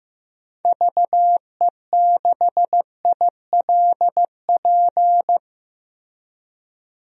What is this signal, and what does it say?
Morse code, heard as a keyed tone, about 15 wpm, 700 Hz. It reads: VE6ILP
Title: DXCC-100 Call Signs - 6 Characters - Repeated 2x - 15wpm